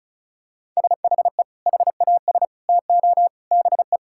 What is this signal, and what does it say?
Morse code, heard as a keyed tone, about 36 wpm, 700 Hz. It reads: SHE HAS TO BE
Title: Sentences from Top 1600 Words 35wpm